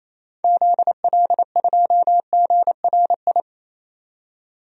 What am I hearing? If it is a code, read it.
Morse code, heard as a keyed tone, about 28 wpm, 700 Hz. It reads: ZL2GRI
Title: DXCC-100 Call Signs - 6 Characters - 28wpm